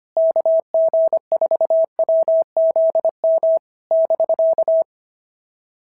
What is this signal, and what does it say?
Morse code, heard as a keyed tone, about 25 wpm, 650 Hz. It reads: KG4WZM <BK>
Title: SOTA QSO Practice 25wpm